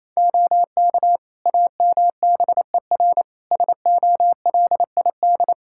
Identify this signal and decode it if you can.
Morse code, heard as a keyed tone, about 28 wpm, 700 Hz. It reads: OK AMBER SOLID
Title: Practice QSO - Exchange 3 - More Info - 28wpm